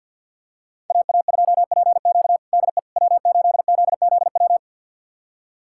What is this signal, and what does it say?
Morse code, heard as a keyed tone, about 50 wpm, 700 Hz. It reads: AA2PQ DE W8ZZW
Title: USA Callsign DE Callsign 50wpm